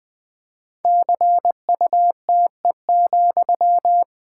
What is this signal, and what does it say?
Morse code, heard as a keyed tone, about 20 wpm, 700 Hz. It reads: CUTE,
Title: Pangram Challenge v2 - 20wpm